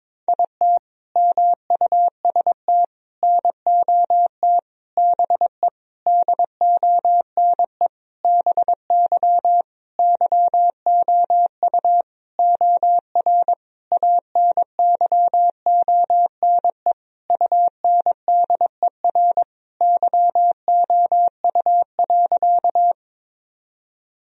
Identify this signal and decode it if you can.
Morse code, heard as a keyed tone, about 22 wpm, 700 Hz. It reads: IT MUST NOT BE DONE BY YOU OR ANYONE UNDER YOU.
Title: Sentences from Top 1000 Words - Review Part 1 - 22wpm